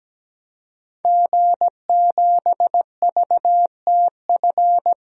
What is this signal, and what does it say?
Morse code, heard as a keyed tone, about 17 wpm, 700 Hz. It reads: G7VTF
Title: DXCC-100 Call Signs - 5 Characters - 17wpm